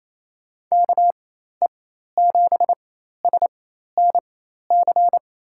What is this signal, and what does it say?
Morse code, heard as a keyed tone, about 28 wpm, spaced as for 13 wpm, 700 Hz. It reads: KE7SNC
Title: USA Call Signs - 6 Characters - 4x Character Spacing - 28wpm